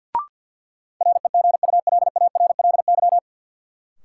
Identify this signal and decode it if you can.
Morse code, heard as a keyed tone, about 53 wpm, 700 Hz. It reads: REGULARLY
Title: Sentences from Top 1400 Words - New Word Drill - 50wpm